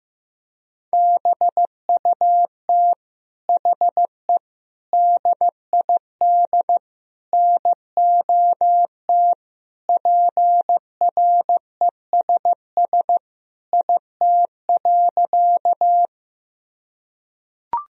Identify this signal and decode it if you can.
Morse code, heard as a keyed tone, about 16 wpm, 700 Hz. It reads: BUT HE DID NOT PRESS IT.
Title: Sentences from Top 1000 Words - Review Part 1 - 15wpm